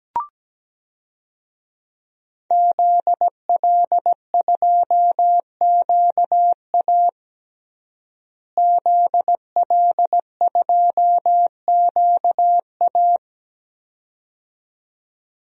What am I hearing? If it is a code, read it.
Morse code, heard as a keyed tone, about 17 wpm, 700 Hz. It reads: ZL2QA ZL2QA
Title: DXCC-100 Call Signs - 5 Characters - Repeated 2x - 17wpm